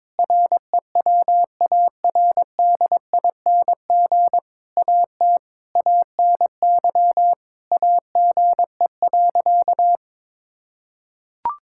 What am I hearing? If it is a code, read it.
Morse code, heard as a keyed tone, about 22 wpm, 700 Hz. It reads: REWARDING AT ANY AGE.
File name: Difficult Sentences with Hints - 22wpm